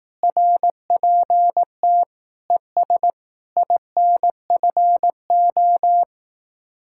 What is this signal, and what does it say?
Morse code, heard as a keyed tone, about 18 wpm, 700 Hz. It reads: RPT ES INFO
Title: Speed-Racing Common QSO Phrases Encore - 15wpm